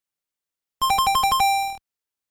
There is no code.